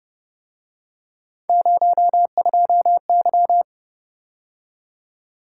Morse code, heard as a keyed tone, about 30 wpm, 700 Hz. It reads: 02Y